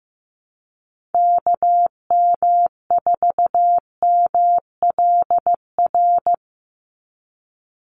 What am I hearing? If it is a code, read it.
Morse code, heard as a keyed tone, about 15 wpm, 700 Hz. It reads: KM4MLR
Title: USA Call Signs 15wpm